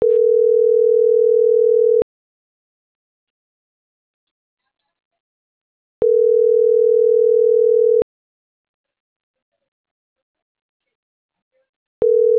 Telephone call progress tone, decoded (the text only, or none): ringback tone